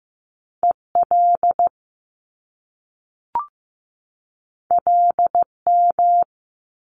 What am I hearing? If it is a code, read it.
Morse code, heard as a keyed tone, about 15 wpm, 700 Hz. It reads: EL LM
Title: Top Two Letter Combinations in Words 15wpm